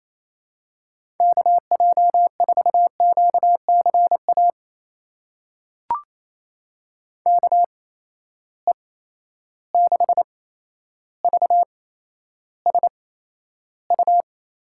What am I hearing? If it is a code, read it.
Morse code, heard as a keyed tone, about 28 wpm, spaced as for 12 wpm, 700 Hz. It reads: KJ4QCA KE6VSU
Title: USA Call Signs - 6 Characters - 8x Character Spacing - 28wpm